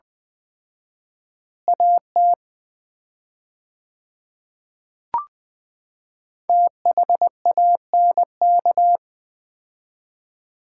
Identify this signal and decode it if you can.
Morse code, heard as a keyed tone, about 20 wpm, 700 Hz. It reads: AT THANK
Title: 4x Word-Spacing - Sets of 3 Words - 20wpm